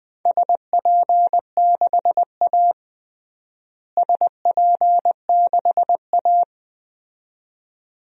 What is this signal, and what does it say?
Morse code, heard as a keyed tone, about 21 wpm, 700 Hz. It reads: SP6A SP6A
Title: DXCC-100 Call Signs - 4 Characters - Repeated 2x - 20wpm